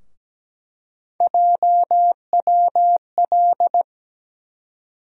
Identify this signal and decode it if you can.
Morse code, heard as a keyed tone, about 17 wpm, 700 Hz. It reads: JWL
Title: USA Call Sign Postfixes - 3 Letters 17wpm